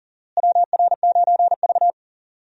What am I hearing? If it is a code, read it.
Morse code, heard as a keyed tone, about 37 wpm, 700 Hz. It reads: WR9V
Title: USA Call Signs - 4 Characters - Repeated 3x - 40wpm